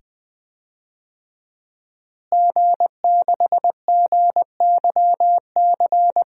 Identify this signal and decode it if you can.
Morse code, heard as a keyed tone, about 20 wpm, 700 Hz. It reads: G6GYC
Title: DXCC-100 Call Signs - 5 Characters - Repeated 2x - 20wpm